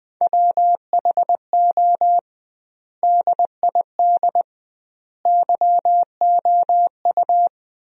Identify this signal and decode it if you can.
Morse code, heard as a keyed tone, about 20 wpm, 700 Hz. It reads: WHO DID YOU
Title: 2x Word-Spacing - Sentences from Top 300 Words - 20wpm